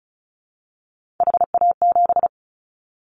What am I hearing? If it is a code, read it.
Morse code, heard as a keyed tone, about 34 wpm, 700 Hz. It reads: HA7
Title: Callsign Prefixes for Top 25 DXCC - 35wpm